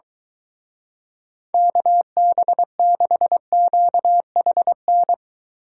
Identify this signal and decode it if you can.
Morse code, heard as a keyed tone, about 23 wpm, 700 Hz. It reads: KB6QHN